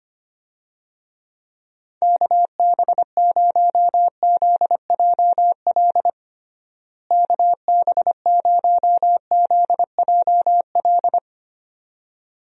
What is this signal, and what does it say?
Morse code, heard as a keyed tone, about 25 wpm, 700 Hz. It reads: KB0ZJL KB0ZJL
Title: USA Call Signs - 6 Characters - Repeated 2x - 25wpm